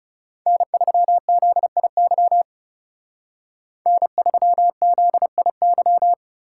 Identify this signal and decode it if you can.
Morse code, heard as a keyed tone, about 37 wpm, 700 Hz. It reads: N3ZIY N3ZIY
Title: Speed Racing - USA 1x3 Call Signs 30wpm